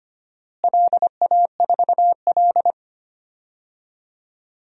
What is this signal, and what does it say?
Morse code, heard as a keyed tone, about 25 wpm, 700 Hz. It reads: LA4L
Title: DXCC-100 Call Signs - 4 Characters - Repeated 3x - 25wpm